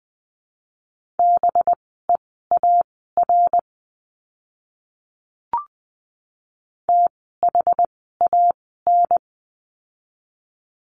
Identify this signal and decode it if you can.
Morse code, heard as a keyed tone, about 20 wpm, spaced as for 14 wpm, 700 Hz. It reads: BEAR THAN
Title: 185 Lesson 30f - Letter b - Words 20wpm